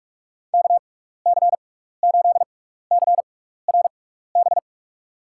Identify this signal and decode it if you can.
Morse code, heard as a keyed tone, about 43 wpm, spaced as for 15 wpm, 700 Hz. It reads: KC8CRB
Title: USA Call Signs - 6 Characters - 6x Character Spacing - 45wpm